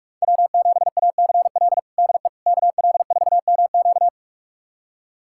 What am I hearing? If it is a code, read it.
Morse code, heard as a keyed tone, about 46 wpm, 700 Hz. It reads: W7ACL DE KL4MQ
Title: USA Callsign DE Callsign - 5 Characters - 45wpm